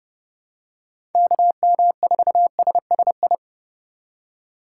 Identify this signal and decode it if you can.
Morse code, heard as a keyed tone, about 30 wpm, 700 Hz. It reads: KM4SSI
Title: USA Call Signs - 6 Characters - 2x Character Spacing - 30wpm